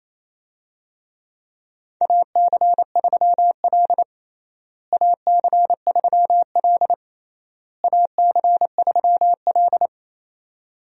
Morse code, heard as a keyed tone, about 28 wpm, 700 Hz. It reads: AC3L AC3L AC3L
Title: USA Call Signs - 4 Characters - Repeated 3x - 28wpm